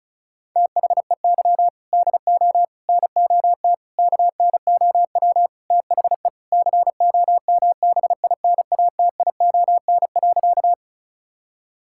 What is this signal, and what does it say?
Morse code, heard as a keyed tone, about 34 wpm, 700 Hz. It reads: THEY DO NOT KNOW THE COMBINATION.
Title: Sentences from Top 1000 Words - Review Part 6 - 35wpm